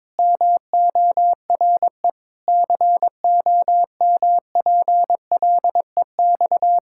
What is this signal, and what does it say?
Morse code, heard as a keyed tone, about 22 wpm, 700 Hz. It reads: MORE COMPLEX
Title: Sentences from Top 1000 Words - Review Part 6 - 22wpm